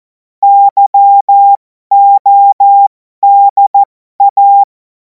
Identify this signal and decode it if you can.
Morse code, heard as a keyed tone, about 14 wpm, spaced as for 12 wpm, 800 Hz. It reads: YODA
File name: Morsekurs / Witze